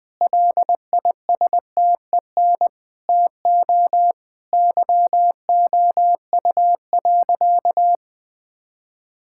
Morse code, heard as a keyed tone, about 20 wpm, 700 Hz. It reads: LISTEN TO YOU.
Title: Sentences from Top 1600 Words 20wpm